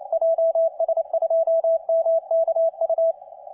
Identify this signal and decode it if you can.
Morse code, heard as a keyed tone, about 31 wpm, 660 Hz, with radio noise. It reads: JS2MKU